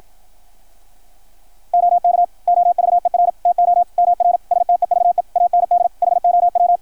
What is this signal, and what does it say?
Morse code, heard as a keyed tone, about 53 wpm, 660 Hz, with radio noise. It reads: OK OVER TO MR STEVE AND HOW
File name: archived / CRSnet